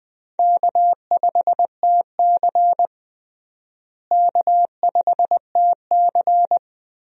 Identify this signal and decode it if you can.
Morse code, heard as a keyed tone, about 20 wpm, 700 Hz. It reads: K5TC K5TC
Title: USA Call Signs - 4 Characters - Repeated 2x - 20wpm